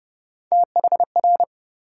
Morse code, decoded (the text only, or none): THR